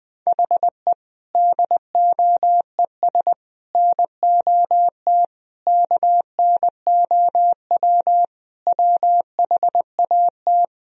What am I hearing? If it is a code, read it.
Morse code, heard as a keyed tone, about 21 wpm, 700 Hz. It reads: HE DOES NOT KNOW WHAT
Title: Sentences from Top 1000 Words - Review Part 3 - 20wpm